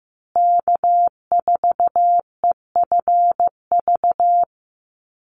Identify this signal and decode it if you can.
Morse code, heard as a keyed tone, about 15 wpm, 700 Hz. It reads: K4EFV
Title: USA Call Signs Repeated 3x 15wpm